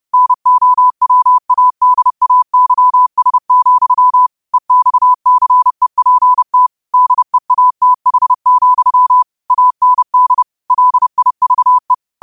Morse code, heard as a keyed tone, about 31 wpm, 1000 Hz. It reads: NOWADAYS, EXCEPT DEATH, AND LIVE
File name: 30wpm